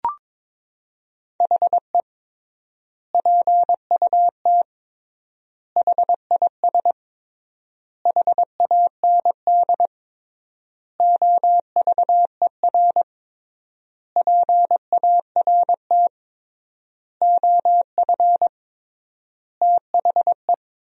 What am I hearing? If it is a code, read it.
Morse code, heard as a keyed tone, about 21 wpm, 700 Hz. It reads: HE PUT HIS HAND OVER PART OF THE
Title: 3x Word-Spacing - Sentences from Top 400 Words - 22wpm